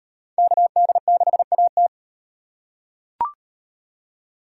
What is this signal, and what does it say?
Morse code, heard as a keyed tone, about 38 wpm, 700 Hz. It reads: KD6AT